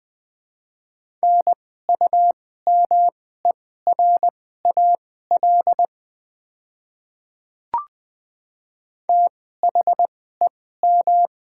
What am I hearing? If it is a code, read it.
Morse code, heard as a keyed tone, about 20 wpm, spaced as for 14 wpm, 700 Hz. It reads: NUMERAL THEM
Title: 2x Character Spacing - Top 500 Words - 20wpm